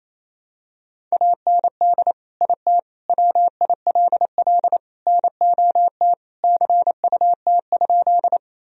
Morse code, heard as a keyed tone, about 27 wpm, 700 Hz. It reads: AND IT WILL NOT CUT?
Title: Sentences from Top 1000 Words - Review Part 1 - 28wpm